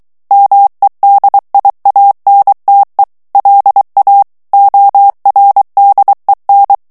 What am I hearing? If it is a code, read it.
Morse code, heard as a keyed tone, about 23 wpm, 800 Hz. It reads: MEDIANTE LA ORDEN
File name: morse mp3 / ESPAGNOL TEXTE EN CLAIR 2 VITESSE 22 MOTS MINUTES